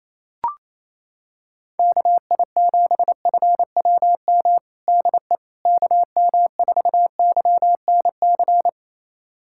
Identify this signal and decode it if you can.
Morse code, heard as a keyed tone, about 28 wpm, 700 Hz. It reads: KI7FWM DE KM4YNC
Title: USA Callsign DE Callsign - 6 Characters - 28wpm